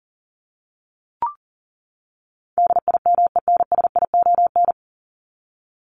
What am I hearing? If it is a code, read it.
Morse code, heard as a keyed tone, about 40 wpm, 700 Hz. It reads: DIMENSION
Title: Top 1501-1750 Words 40wpm